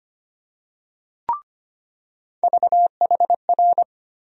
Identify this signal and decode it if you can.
Morse code, heard as a keyed tone, about 25 wpm, 700 Hz. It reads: VHR
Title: USA Call Sign Postfixes - 3 Letters 25wpm